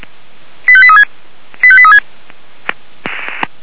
Five-tone selective call (ZVEI), heard as quick runs of five tones, tones of about 80 ms. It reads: 85836 85836